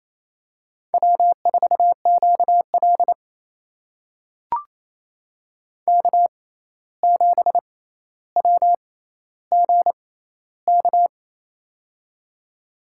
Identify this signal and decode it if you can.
Morse code, heard as a keyed tone, about 28 wpm, spaced as for 13 wpm, 700 Hz. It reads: W4QL K7WGK
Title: USA Call Signs - 6x Character Spacing - 28wpm